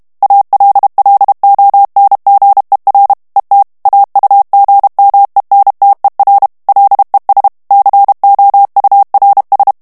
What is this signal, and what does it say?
Morse code, heard as a keyed tone, about 32 wpm, 800 Hz. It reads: ALLONGER ET AUGMENTER LES COURS
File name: morse mp3 / FRANCAIS TEXTE CLAIR 1 VITESSE 30 MOTS MINUTES